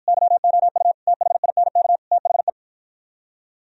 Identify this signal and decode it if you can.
Morse code, heard as a keyed tone, about 50 wpm, 700 Hz. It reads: YOU THINK THE